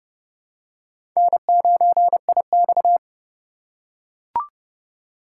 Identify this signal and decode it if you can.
Morse code, heard as a keyed tone, about 30 wpm, 700 Hz. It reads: N9IX
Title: USA Call Signs - 4 Characters - 3x Character Spacing - 30wpm